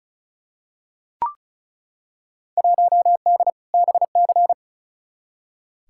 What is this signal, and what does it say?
Morse code, heard as a keyed tone, about 36 wpm, 700 Hz. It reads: 1D BC
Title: Field Day Exchanges - 35wpm